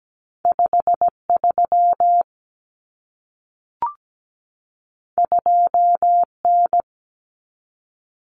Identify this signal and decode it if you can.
Morse code, heard as a keyed tone, about 17 wpm, 700 Hz. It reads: 53 2N
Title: Common Cut Numbers - 2-Digit 17wpm